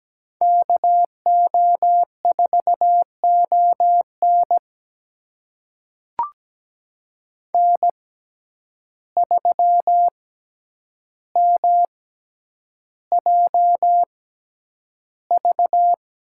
Morse code, heard as a keyed tone, about 17 wpm, spaced as for 9 wpm, 700 Hz. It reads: KO4ON N3MJV
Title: USA Call Signs - 5 Characters - 6x Character Spacing - 17wpm